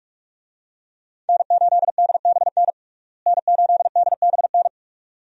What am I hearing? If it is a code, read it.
Morse code, heard as a keyed tone, about 46 wpm, 700 Hz. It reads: N8DBN N8DBN
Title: USA Call Signs - 5 Characters - Repeated 2x - 45wpm